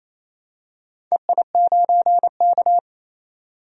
Morse code, heard as a keyed tone, about 29 wpm, 690 Hz. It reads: EI9K
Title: DXCC-100 Call Signs - 4 Characters - 28wpm